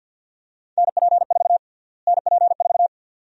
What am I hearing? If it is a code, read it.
Morse code, heard as a keyed tone, about 50 wpm, 700 Hz. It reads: NP4 NP4
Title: USA Call Sign Prefixes - Repeated 2x - 50wpm